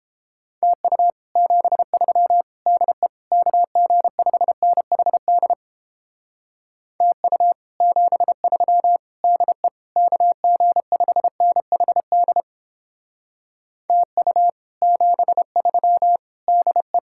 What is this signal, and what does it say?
Morse code, heard as a keyed tone, about 29 wpm, 700 Hz. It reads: TU 73 DE KG5NHD TU 73 DE KG5NHD TU 73 DE